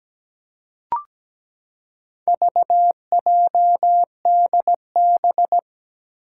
Morse code, heard as a keyed tone, about 17 wpm, 700 Hz. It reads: VJDB
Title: Four Challenging Letter-Number Combinations 17wpm